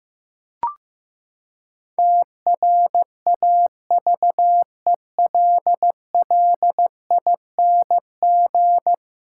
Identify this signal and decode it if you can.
Morse code, heard as a keyed tone, about 15 wpm, 700 Hz. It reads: TRAVELLING
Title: Pangram Challenge v2 - 15wpm